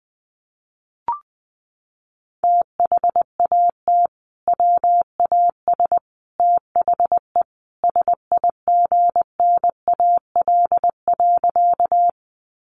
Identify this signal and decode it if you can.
Morse code, heard as a keyed tone, about 20 wpm, 700 Hz. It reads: THAT WAS THE SIGNAL.
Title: Sentences from Top 1400 Words 20wpm